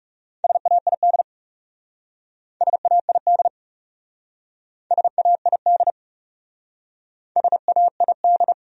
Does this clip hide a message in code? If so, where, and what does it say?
Yes: Morse code from 0.4 to 8.5 s, heard as a keyed tone, about 39 wpm, 700 Hz. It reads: SAID SAID SAID SAID